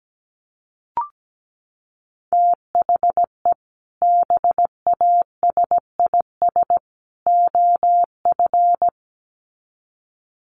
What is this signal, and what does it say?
Morse code, heard as a keyed tone, about 17 wpm, 700 Hz. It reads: THE BASIS OF
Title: Sets of 3 Words 17wpm